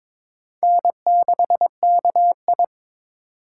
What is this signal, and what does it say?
Morse code, heard as a keyed tone, about 22 wpm, 700 Hz. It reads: N6KI
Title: USA Call Signs - 4 Characters - 8x Character Spacing - 22wpm